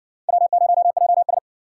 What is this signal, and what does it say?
Morse code, heard as a keyed tone, about 60 wpm, 700 Hz. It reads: W0JS